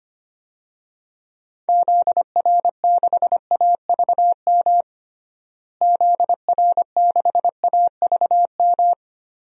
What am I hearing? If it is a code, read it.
Morse code, heard as a keyed tone, about 25 wpm, 700 Hz. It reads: ZR6AVM ZR6AVM
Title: DXCC-100 Call Signs - 6 Characters - Repeated 3x - 25wpm